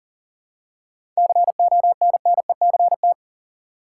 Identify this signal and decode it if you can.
Morse code, heard as a keyed tone, about 40 wpm, 700 Hz. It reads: CONNECT
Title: Seven Letter Words - 40wpm